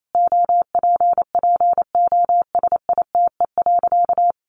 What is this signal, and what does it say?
Morse code, heard as a keyed tone, about 28 wpm, 700 Hz. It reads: OPPOSITE.
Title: Sentences from Top 1300 Words 28wpm